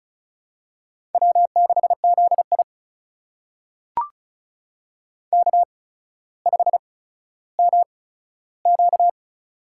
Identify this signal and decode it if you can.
Morse code, heard as a keyed tone, about 36 wpm, spaced as for 15 wpm, 700 Hz. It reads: W6ZI K5MQ
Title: USA Call Signs - 4 Characters - 8x Character Spacing - 35wpm